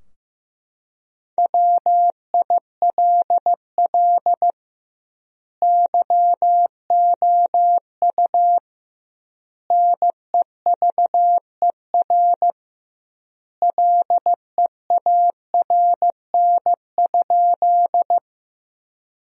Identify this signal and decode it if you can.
Morse code, heard as a keyed tone, about 15 wpm, 700 Hz. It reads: WILL YOU NEVER LEARN?
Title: 2x Word-Spacing - Sentences from Top 300 Words - 15wpm